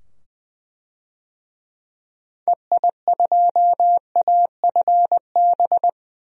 Morse code, heard as a keyed tone, about 21 wpm, 700 Hz. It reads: EI2AFB
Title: DXCC-100 Call Signs - 6 Characters - 20wpm